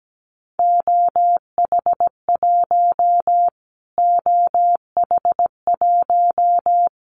Morse code, heard as a keyed tone, about 17 wpm, 700 Hz. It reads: OH1 OH1
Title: Callsign Prefixes for Top 25 DXCC - Repeated 3x - 17wpm